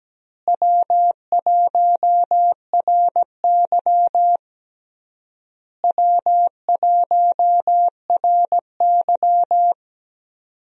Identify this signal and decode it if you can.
Morse code, heard as a keyed tone, about 17 wpm, 700 Hz. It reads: W1RY W1RY
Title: USA Call Signs - 4 Characters - Repeated 3x - 17wpm